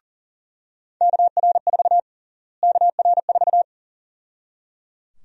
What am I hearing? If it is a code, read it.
Morse code, heard as a keyed tone, about 37 wpm, 700 Hz. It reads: KR4 KR4
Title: USA Call Sign Prefixes - Repeated 2x - 40wpm